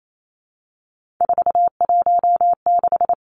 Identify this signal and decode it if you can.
Morse code, heard as a keyed tone, about 28 wpm, 700 Hz. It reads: 416
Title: Numbers - 3-Digit 28wpm